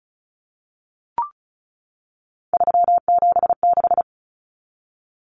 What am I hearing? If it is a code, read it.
Morse code, heard as a keyed tone, about 34 wpm, 700 Hz. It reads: 376
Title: Common Cut Numbers - 3-Digit 35wpm